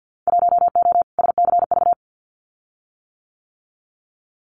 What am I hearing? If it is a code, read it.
Morse code, heard as a keyed tone, about 52 wpm, 700 Hz. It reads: 1O SCV
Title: Winter Field Day Exchanges 50wpm